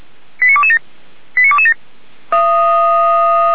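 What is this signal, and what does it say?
Five-tone selective call (ZVEI), heard as quick runs of five tones, tones of about 70 ms. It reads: 89227 89227
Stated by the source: ZVEI five-tone selective call 85385 received on an SDR.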